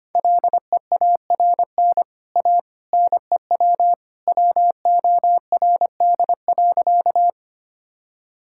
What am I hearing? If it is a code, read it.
Morse code, heard as a keyed tone, about 25 wpm, 700 Hz. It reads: LEARN A NEW WORD.
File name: Easy Sentences with Hints - 25wpm